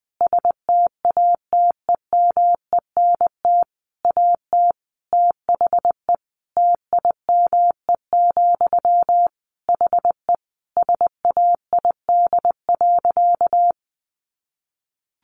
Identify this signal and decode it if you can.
Morse code, heard as a keyed tone, about 20 wpm, 700 Hz. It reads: STATEMENT AT THE TIME, HE SAID.
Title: Sentences from Top 1200 Words 20wpm